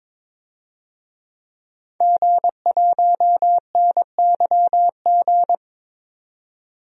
Morse code, heard as a keyed tone, about 22 wpm, 700 Hz. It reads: G1NYG
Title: DXCC-100 Call Signs - 5 Characters - 22wpm